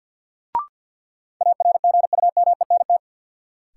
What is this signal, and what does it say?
Morse code, heard as a keyed tone, about 50 wpm, 700 Hz. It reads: ARGUMENT
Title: Sentences from Top 1400 Words - New Word Drill Alt - 50wpm